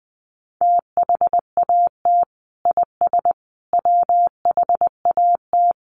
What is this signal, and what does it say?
Morse code, heard as a keyed tone, about 20 wpm, 700 Hz. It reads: THAT IS WHAT
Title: Sentences from Top 100 Words 20wpm